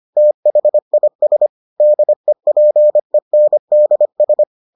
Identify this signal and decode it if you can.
Morse code, heard as a keyed tone, about 25 wpm, 600 Hz. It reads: THIS DEPENDS